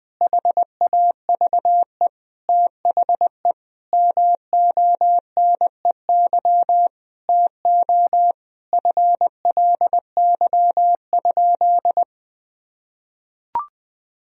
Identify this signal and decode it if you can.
Morse code, heard as a keyed tone, about 20 wpm, 700 Hz. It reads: HAVE THE MONEY TO FLY?
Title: Sentences from Top 1000 Words - Review Part 2 - 20wpm